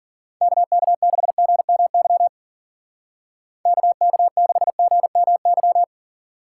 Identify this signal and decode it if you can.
Morse code, heard as a keyed tone, about 51 wpm, 700 Hz. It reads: KK6GMY KK6GMY